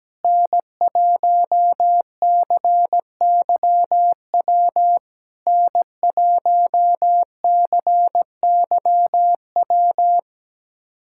Morse code, heard as a keyed tone, about 17 wpm, 700 Hz. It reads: N1CYW N1CYW
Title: SOTA QSO Practice 17wpm